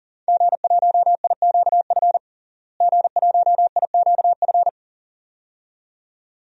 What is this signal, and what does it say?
Morse code, heard as a keyed tone, about 40 wpm, 700 Hz. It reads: G1IQF G1IQF
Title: DXCC-100 Call Signs - 5 Characters - Repeated 2x - 40wpm